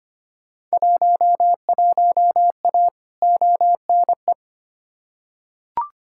Morse code, heard as a keyed tone, about 25 wpm, 700 Hz. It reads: 11A ONE
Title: Field Day Exchanges Encore - 25wpm